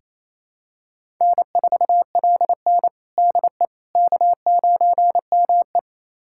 Morse code, heard as a keyed tone, about 27 wpm, 700 Hz. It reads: N4LN DE K9ME
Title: USA Callsign DE Callsign - 4 Characters - 28wpm